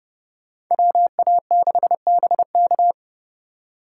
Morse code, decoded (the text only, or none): WA6BK